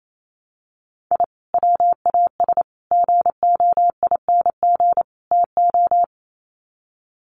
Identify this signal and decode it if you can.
Morse code, heard as a keyed tone, about 28 wpm, 700 Hz. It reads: I WAS GOING TO
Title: Sets of 4 Words 28wpm